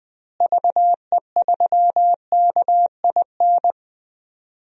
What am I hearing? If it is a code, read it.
Morse code, heard as a keyed tone, about 20 wpm, 700 Hz. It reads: VE3KIN
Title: DXCC-100 Call Signs - 6 Characters - Repeated 3x - 20wpm